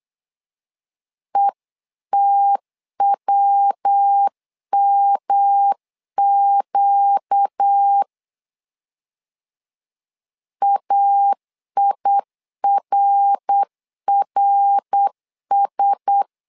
Morse code, heard as a keyed tone, about 8 wpm, 800 Hz. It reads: ETWMQ AIRRS
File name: LEZIONI / 4 Q1S 8wpm 2710